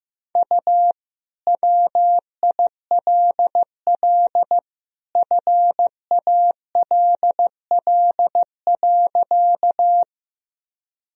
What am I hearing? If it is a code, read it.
Morse code, heard as a keyed tone, about 15 wpm, 700 Hz. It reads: U WILL FALL.
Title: Sentences from Top 500 Words and Abbreviations 15wpm